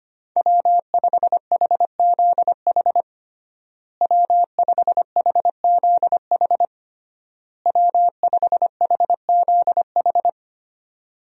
Morse code, heard as a keyed tone, about 25 wpm, 700 Hz. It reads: W5HZH W5HZH W5HZH
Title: USA Call Signs - 5 Characters - Repeated 3x - 25wpm